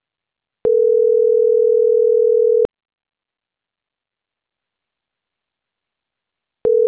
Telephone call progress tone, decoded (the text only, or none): ringback tone